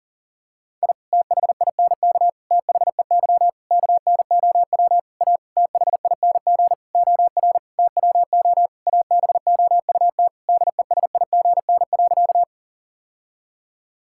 Morse code, heard as a keyed tone, about 37 wpm, 700 Hz. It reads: I THINK THEY KNOW A THING OR TWO ABOUT DESIGN.
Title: Sentences from Top 1000 Words - Review Part 5 - 40wpm